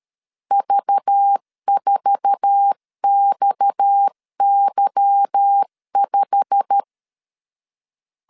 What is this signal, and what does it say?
Morse code, heard as a keyed tone, about 13 wpm, 800 Hz. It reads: V4XY5
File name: LEZIONI / 8 YJ45 / 8 YJ45 12wpm 2151